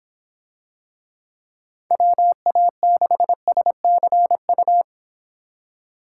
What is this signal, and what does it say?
Morse code, heard as a keyed tone, about 24 wpm, 700 Hz. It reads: WA6SCU